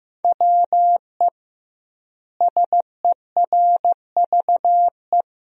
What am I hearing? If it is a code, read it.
Morse code, heard as a keyed tone, about 15 wpm, 700 Hz. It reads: WE SERVE
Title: 2x Word-Spacing - Sentences from Top 400 Words - 15wpm